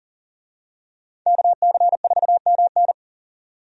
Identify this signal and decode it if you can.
Morse code, heard as a keyed tone, about 39 wpm, 700 Hz. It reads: KC4MN
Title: USA Call Signs - 5 Characters - 40wpm